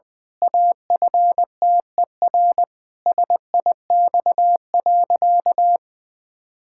Morse code, heard as a keyed tone, about 20 wpm, 700 Hz. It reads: AFTER SIX.
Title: Pangram Challenge v2 - 20wpm